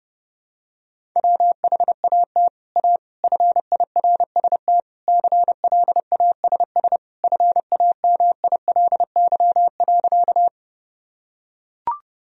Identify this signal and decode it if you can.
Morse code, heard as a keyed tone, about 32 wpm, 700 Hz. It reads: WHAT A FIRST CLASS FAMILY.
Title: Sentences from Top 1000 Words - Review Part 2 - 30wpm